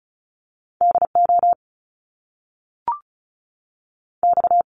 Morse code, heard as a keyed tone, about 36 wpm, spaced as for 6 wpm, 700 Hz. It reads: DOX
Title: Top Two Letter Combinations in Words 35wpm